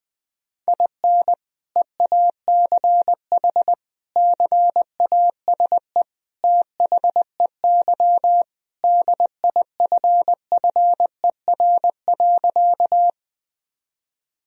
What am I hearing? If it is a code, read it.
Morse code, heard as a keyed tone, about 19 wpm, 700 Hz. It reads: IN EACH CASE THEY DIFFER.
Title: Sentences from Top 1000 Words - Review Part 1 - 20wpm